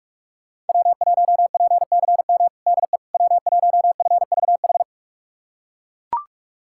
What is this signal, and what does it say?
Morse code, heard as a keyed tone, about 41 wpm, 700 Hz. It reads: W1PCM DE W1FVH
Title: USA Callsign DE Callsign - 5 Characters - 45wpm